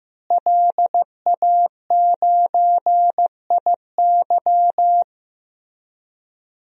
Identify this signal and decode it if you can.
Morse code, heard as a keyed tone, about 15 wpm, 700 Hz. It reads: LA9IY
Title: DXCC-100 Call Signs - 5 Characters - Repeated 2x - 15wpm